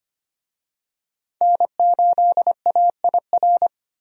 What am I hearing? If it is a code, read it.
Morse code, heard as a keyed tone, about 25 wpm, 700 Hz. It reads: N8AIR
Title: USA Call Signs - 5 Characters - 25wpm